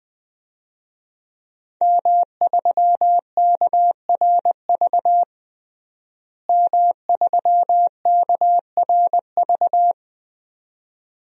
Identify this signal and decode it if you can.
Morse code, heard as a keyed tone, about 20 wpm, 700 Hz. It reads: M3KRV M3KRV
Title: DXCC-100 Call Signs - 5 Characters - Repeated 2x - 20wpm